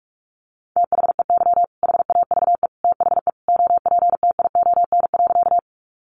Morse code, heard as a keyed tone, about 46 wpm, 700 Hz. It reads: THEY HAVE THE OPTION.
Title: Sentences from Top 1300 Words 45wpm